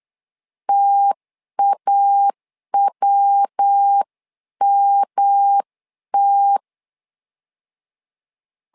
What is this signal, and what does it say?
Morse code, heard as a keyed tone, about 8 wpm, 800 Hz. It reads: TAWMT
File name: LEZIONI / 3 MWRO / 3 MW 8wpm 2501